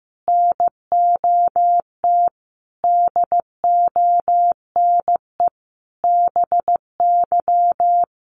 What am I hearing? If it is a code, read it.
Morse code, heard as a keyed tone, about 15 wpm, 700 Hz. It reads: NOT DONE BY
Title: Sentences from Top 1300 Words 15wpm